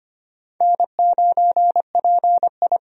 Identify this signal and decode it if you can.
Morse code, heard as a keyed tone, about 25 wpm, 700 Hz. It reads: N9PI
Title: USA Call Signs - 4 Characters - 25wpm